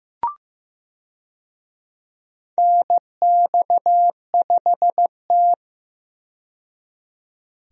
Morse code, heard as a keyed tone, about 15 wpm, 700 Hz. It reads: NX5T
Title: USA Call Signs - 4 Characters - 15wpm